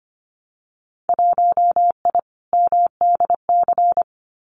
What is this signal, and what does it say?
Morse code, heard as a keyed tone, about 25 wpm, 700 Hz. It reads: 1I MDC
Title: Winter Field Day Exchanges 25wpm